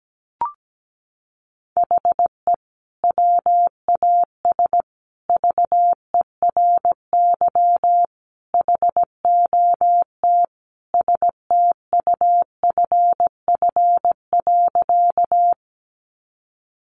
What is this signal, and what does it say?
Morse code, heard as a keyed tone, about 17 wpm, 700 Hz. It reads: HE WAS VERY HOT STUFF.
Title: Sentences from Top 1400 Words 17wpm